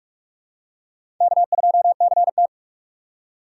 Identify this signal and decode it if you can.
Morse code, heard as a keyed tone, about 44 wpm, 700 Hz. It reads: K2CT
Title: USA Call Signs - 6x Character Spacing - 45wpm